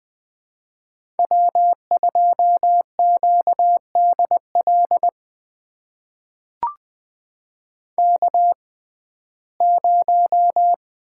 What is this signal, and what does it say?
Morse code, heard as a keyed tone, about 20 wpm, spaced as for 14 wpm, 700 Hz. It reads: W2QDL K0
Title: USA Call Signs - 5 Characters - 6x Character Spacing - 20wpm